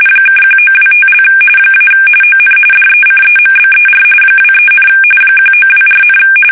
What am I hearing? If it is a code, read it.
RTTY, about 45.45 baud. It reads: (1925) - ON DECEMBER 23, 1924,
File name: fsk850 / teletype-corp